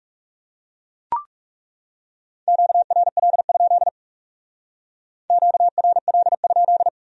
Morse code, heard as a keyed tone, about 37 wpm, 700 Hz. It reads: QRL? QRL?
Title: Speed-Racing Q-Codes - 30wpm